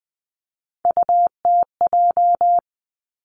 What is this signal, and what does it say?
Morse code, heard as a keyed tone, about 20 wpm, 700 Hz. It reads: UTJ
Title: Three Challenging Letter-Number Combinations 20wpm